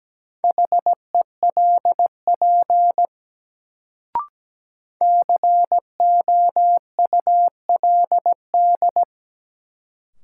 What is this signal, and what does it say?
Morse code, heard as a keyed tone, about 17 wpm, 700 Hz. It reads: HELP COULD
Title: Sets of 2 Words - New Word Drill - 17wpm